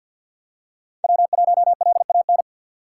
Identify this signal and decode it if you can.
Morse code, heard as a keyed tone, about 51 wpm, 700 Hz. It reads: W1LAN
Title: USA Call Signs - 5 Characters - 6x Character Spacing - 50wpm